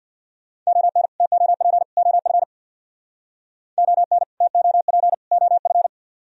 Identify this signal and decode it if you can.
Morse code, heard as a keyed tone, about 62 wpm, 700 Hz. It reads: ON TOP OF ON TOP OF
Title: Speed-Racing - Sets of 3 Words 50wpm